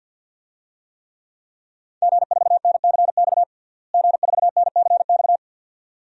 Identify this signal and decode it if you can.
Morse code, heard as a keyed tone, about 52 wpm, 700 Hz. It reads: G4NCX G4NCX
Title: DXCC-100 Call Signs - 5 Characters - Repeated 2x - 50wpm